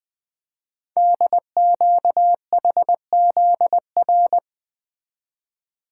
Morse code, heard as a keyed tone, about 20 wpm, 700 Hz. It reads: DQHZR
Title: Code Groups - 5 Letter:Numbers - 20wpm